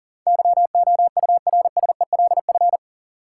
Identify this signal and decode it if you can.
Morse code, heard as a keyed tone, about 40 wpm, 700 Hz. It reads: YOURSELF
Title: Sentences from Top 600 Words - New Word Drill - 40wpm